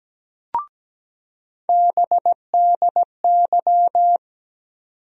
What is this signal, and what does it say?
Morse code, heard as a keyed tone, about 17 wpm, 700 Hz. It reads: BDY
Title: USA Call Sign Postfixes - 3 Letters 17wpm